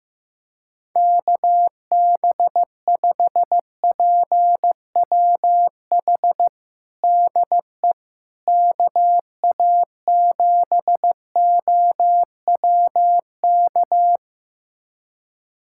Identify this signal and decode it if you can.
Morse code, heard as a keyed tone, about 15 wpm, 700 Hz. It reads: KB5PWH DE KA7OWK
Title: USA Callsign DE Callsign 15wpm